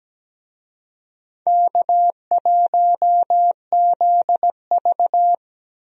Morse code, heard as a keyed tone, about 17 wpm, 700 Hz. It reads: K1ZV
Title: USA Call Signs - 4 Characters - Repeated 2x - 17wpm